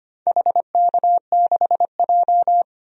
Morse code, heard as a keyed tone, about 25 wpm, 700 Hz. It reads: HK6J
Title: DXCC-100 Call Signs - 4 Characters - 25wpm